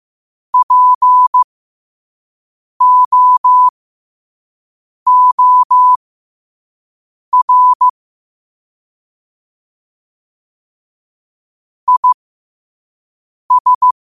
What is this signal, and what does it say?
Morse code, heard as a keyed tone, about 15 wpm, spaced as for 5 wpm, 1000 Hz. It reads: POOR IS
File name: qotdfiles / 5wpm